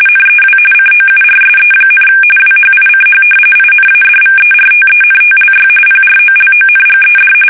RTTY, about 45.45 baud. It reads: NEW COURSE, WHICH INADVERTENTLY PROLONGED
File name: fsk850 / battleshipnj